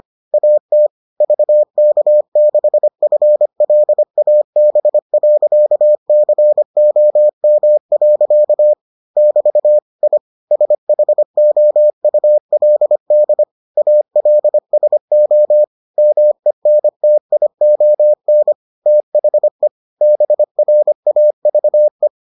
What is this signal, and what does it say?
Morse code, heard as a keyed tone, about 25 wpm, 600 Hz. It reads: AT VK6FLAB.COM. <BT> I SHOULD ALSO MENTION THE BRAVE